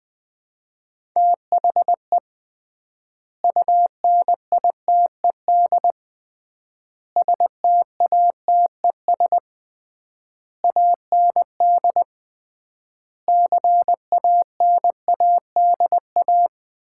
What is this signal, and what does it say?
Morse code, heard as a keyed tone, about 21 wpm, 700 Hz. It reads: THE UNITED STATES AND CANADA
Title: 3x Word-Spacing - Sets of 5 Words - 20wpm